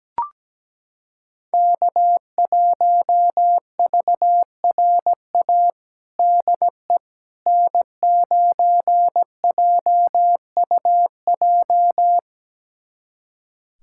Morse code, heard as a keyed tone, about 17 wpm, 700 Hz. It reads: K1VRA DE N9JUJ
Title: USA Callsign DE Callsign - 5 Characters - 17wpm